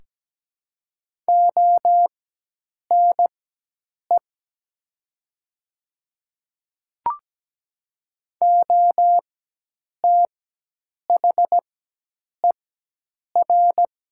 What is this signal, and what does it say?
Morse code, heard as a keyed tone, about 17 wpm, spaced as for 8 wpm, 700 Hz. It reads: ONE OTHER
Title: 4x Character Spacing - Top 100 Words - 17wpm